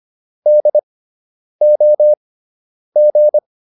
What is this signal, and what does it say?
Morse code, heard as a keyed tone, about 25 wpm, spaced as for 9 wpm, 600 Hz. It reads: DOG